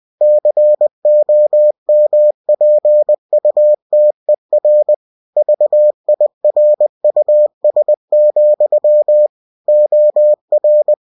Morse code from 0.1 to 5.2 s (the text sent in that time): COMPUTER